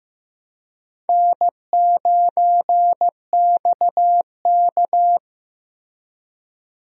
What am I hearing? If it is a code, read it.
Morse code, heard as a keyed tone, about 14 wpm, 700 Hz. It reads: N9XK
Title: USA Call Signs - 4 Characters - Repeated 3x - 15wpm